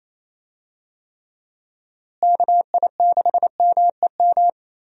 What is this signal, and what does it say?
Morse code, heard as a keyed tone, about 28 wpm, 700 Hz. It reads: KI6MEM
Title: USA Call Signs - 6 Characters - Repeated 3x - 28wpm